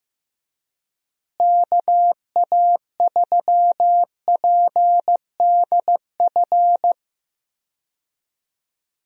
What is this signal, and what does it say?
Morse code, heard as a keyed tone, about 15 wpm, 700 Hz. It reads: KA3PDF
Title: USA Call Signs - 6 Characters - Repeated 3x - 15wpm